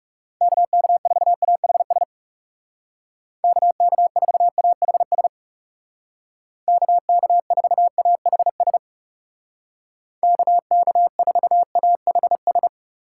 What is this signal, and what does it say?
Morse code, heard as a keyed tone, about 39 wpm, 700 Hz. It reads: KK4AHS KK4AHS KK4AHS KK4AHS